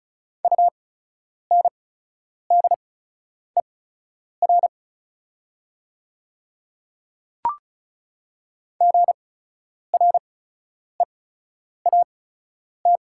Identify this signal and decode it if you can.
Morse code, heard as a keyed tone, about 37 wpm, spaced as for 10 wpm, 700 Hz. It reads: UNDER GREAT
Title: 8x Character Spacing - Top 200 Words - 35wpm